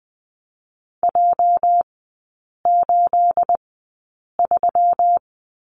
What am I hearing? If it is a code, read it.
Morse code, heard as a keyed tone, about 20 wpm, spaced as for 8 wpm, 690 Hz. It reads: J83
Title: 212 Lesson 34e - Number 8 - Three Characters 20wpm